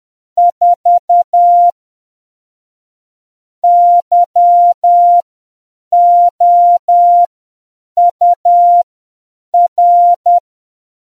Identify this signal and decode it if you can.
Morse code, heard as a keyed tone, about 10 wpm, spaced as for 7 wpm, 700 Hz. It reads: 4 YOUR